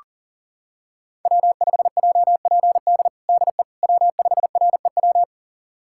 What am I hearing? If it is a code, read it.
Morse code, heard as a keyed tone, about 40 wpm, 700 Hz. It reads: W5JPD DE W5REW
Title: USA Callsign DE Callsign - 5 Characters - 40wpm